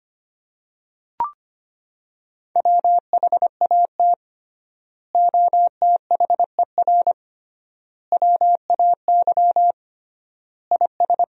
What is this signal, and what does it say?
Morse code, heard as a keyed tone, about 25 wpm, 700 Hz. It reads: WHAT OTHER WAY IS
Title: 3x Word-Spacing - Sentences from Top 100 Words - 25wpm